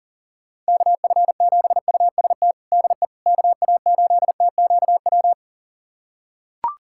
Morse code, heard as a keyed tone, about 37 wpm, 700 Hz. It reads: KF7UST DE KA8TQW
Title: USA Callsign DE Callsign 40wpm